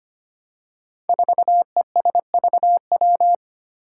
Morse code, heard as a keyed tone, about 25 wpm, 700 Hz. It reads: 4ESVW
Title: Five Challenging Letter-Number Combinations 25wpm